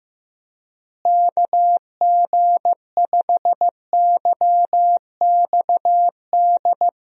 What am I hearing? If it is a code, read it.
Morse code, heard as a keyed tone, about 16 wpm, 700 Hz. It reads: KG5YXD
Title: USA Call Signs - 6 Characters - Repeated 3x - 15wpm